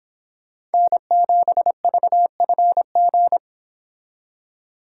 Morse code, heard as a keyed tone, about 26 wpm, 700 Hz. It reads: N7VFG